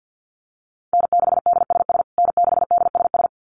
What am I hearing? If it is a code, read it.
Morse code, heard as a keyed tone, about 48 wpm, 700 Hz. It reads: N6DSS N6DSS
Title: USA Call Signs Repeated 2x 50wpm